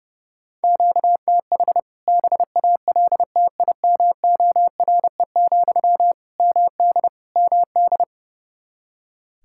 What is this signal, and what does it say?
Morse code, heard as a keyed tone, about 30 wpm, 700 Hz. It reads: QTH BALTIMORE, MD MD
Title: K1USN SST QSO Practice 30wpm